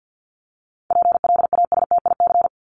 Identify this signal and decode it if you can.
Morse code, heard as a keyed tone, about 50 wpm, 700 Hz. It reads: PLASTIC
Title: Top 1001-1250 Words 50wpm